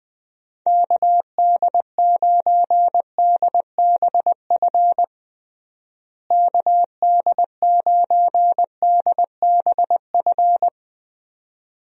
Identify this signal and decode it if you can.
Morse code, heard as a keyed tone, about 21 wpm, 700 Hz. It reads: KD9DBF KD9DBF
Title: USA Call Signs - 6 Characters - Repeated 2x - 20wpm